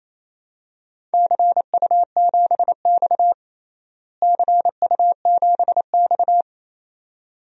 Morse code, heard as a keyed tone, about 28 wpm, 700 Hz. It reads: CU7X CU7X
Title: DXCC-100 Call Signs - 4 Characters - Repeated 2x - 28wpm